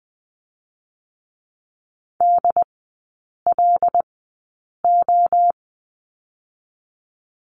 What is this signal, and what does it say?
Morse code, heard as a keyed tone, about 20 wpm, 700 Hz. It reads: D L O
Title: 121 Lesson 20e - Number 3 - Three Characters 20wpm